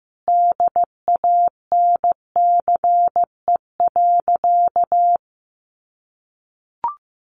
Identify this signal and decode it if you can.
Morse code, heard as a keyed tone, about 15 wpm, 700 Hz. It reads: DANCE.
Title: Sentences from Top 700 Words 15wpm